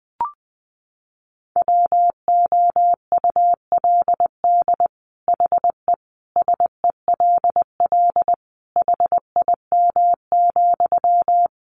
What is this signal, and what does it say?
Morse code, heard as a keyed tone, about 20 wpm, 700 Hz. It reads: WOULD HE SELL HIM,
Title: Sentences from Top 1100 Words 20wpm